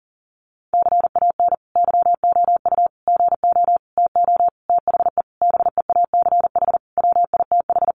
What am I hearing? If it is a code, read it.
Morse code, heard as a keyed tone, about 40 wpm, 700 Hz. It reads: CAN YOU GO TO THE BEACH WITH
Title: Sentences from Top 1500 Words 40wpm